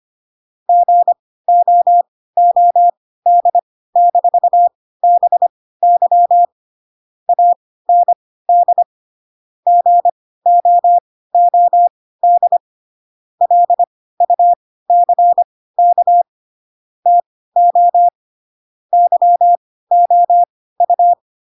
Morse code, heard as a keyed tone, about 25 wpm, spaced as for 16 wpm, 700 Hz. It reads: GOOD-BY AND GOOD LUCK TO YOU